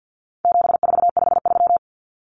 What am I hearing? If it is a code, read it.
Morse code, heard as a keyed tone, about 50 wpm, 700 Hz. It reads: 7453
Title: Numbers - 4-Digit 50wpm